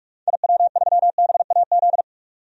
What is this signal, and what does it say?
Morse code, heard as a keyed tone, about 42 wpm, 700 Hz. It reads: IW3BAZ
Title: DXCC-100 Call Signs - 6 Characters - 45wpm